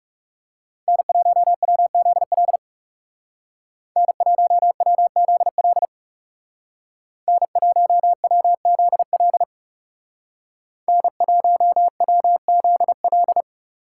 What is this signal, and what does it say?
Morse code, heard as a keyed tone, about 37 wpm, 700 Hz. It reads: N1WZL N1WZL N1WZL N1WZL
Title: Speed Racing - USA 1x3 Call Signs 30wpm